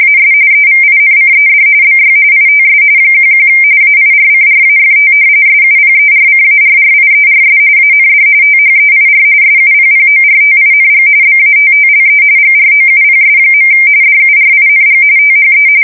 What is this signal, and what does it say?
RTTY, about 45.45 baud. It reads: AS EVINCED BY ITS 11 AIRCRAFT SUPERCARRIERS AND THEIR SUPPORTING BATTLE GROUPS. DURING THE
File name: fsk170 / usnavy5